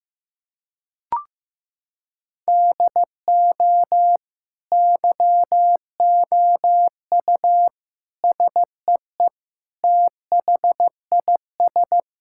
Morse code, heard as a keyed tone, about 15 wpm, 700 Hz. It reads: DO YOU SEE THIS
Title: Sentences from Top 1600 Words 15wpm